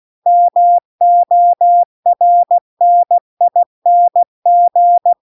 Morse code, heard as a keyed tone, about 16 wpm, 700 Hz. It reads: MORNING
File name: 16WPM